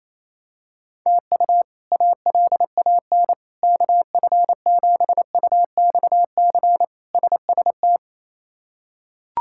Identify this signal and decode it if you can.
Morse code, heard as a keyed tone, about 28 wpm, 700 Hz. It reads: TU ALAN KF7UXC SST
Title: K1USN SST QSO Practice - 28wpm